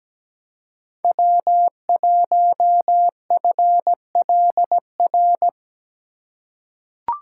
Morse code, heard as a keyed tone, about 17 wpm, 700 Hz. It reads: W1FLR
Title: USA Call Signs - 3x Character Spacing - 17wpm